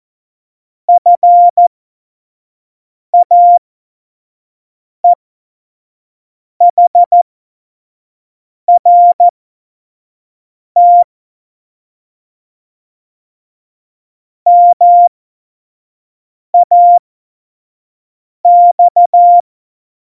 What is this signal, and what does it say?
Morse code, heard as a keyed tone, about 14 wpm, spaced as for 5 wpm, 700 Hz. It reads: FAEHRT MAX
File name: Morsekurs / Witze / Witz 28 Juli 2020, 25 F70